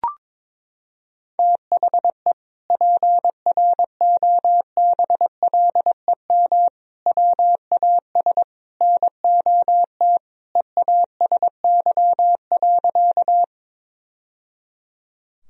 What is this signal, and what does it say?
Morse code, heard as a keyed tone, about 22 wpm, 700 Hz. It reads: THE PROBLEM WAS NOT EASY.
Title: Sentences from Top 1000 Words - Review Part 2 - 22wpm